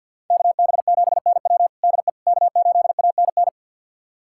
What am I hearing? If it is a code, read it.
Morse code, heard as a keyed tone, about 52 wpm, 700 Hz. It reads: KB7NW DE K8ANN
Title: USA Callsign DE Callsign - 5 Characters - 50wpm